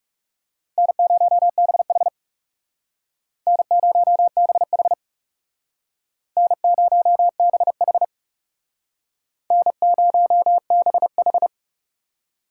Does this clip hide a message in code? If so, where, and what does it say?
Yes: Morse code from 0.8 to 11.5 s, heard as a keyed tone, about 39 wpm, 700 Hz. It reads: N0BH N0BH N0BH N0BH